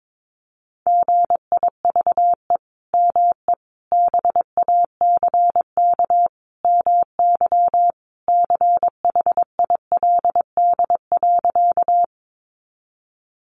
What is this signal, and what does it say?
Morse code, heard as a keyed tone, about 22 wpm, 700 Hz. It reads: GIVE ME BACK MY CHILD.
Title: Sentences from Top 800 Words 22wpm